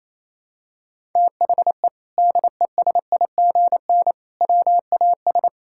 Morse code, heard as a keyed tone, about 28 wpm, 700 Hz. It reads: THE DESIGN WAS
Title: Sentences from Top 1000 Words - Review Part 5 - 28wpm